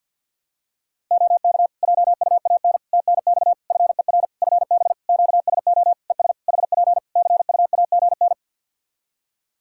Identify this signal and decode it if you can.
Morse code, heard as a keyed tone, about 52 wpm, 700 Hz. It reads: OK JUAN TNX FER FB QSO ES HP CUAGN
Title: Practice QSO - Exchange 4 - Ending - 50wpm